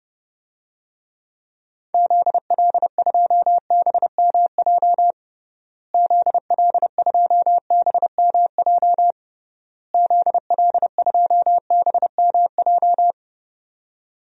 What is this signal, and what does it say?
Morse code, heard as a keyed tone, about 30 wpm, 700 Hz. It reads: ZL2BMJ ZL2BMJ ZL2BMJ
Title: DXCC-100 Call Signs - 6 Characters - Repeated 3x - 30wpm